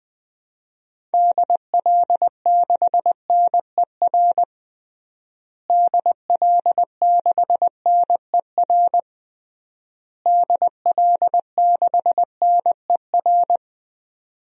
Morse code, heard as a keyed tone, about 20 wpm, 700 Hz. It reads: DL6NER DL6NER DL6NER
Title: DXCC-100 Call Signs - 6 Characters - Repeated 3x - 20wpm